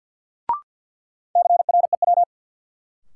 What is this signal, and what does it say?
Morse code, heard as a keyed tone, about 50 wpm, 700 Hz. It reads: CREW
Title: Sentences from Top 1500 Words - New Word Drill - 50wpm